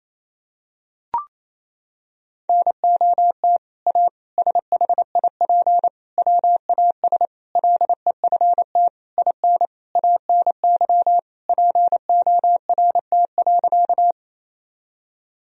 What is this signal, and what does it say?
Morse code, heard as a keyed tone, about 28 wpm, 700 Hz. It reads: NOT A SHIP WAS LEFT IN ANY PORT.
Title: Sentences from Top 1000 Words - Review Part 2 - 28wpm